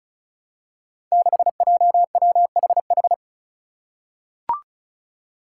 Morse code, heard as a keyed tone, about 36 wpm, 700 Hz. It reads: 6JWHH
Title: Five Challenging Letter-Number Combinations 35wpm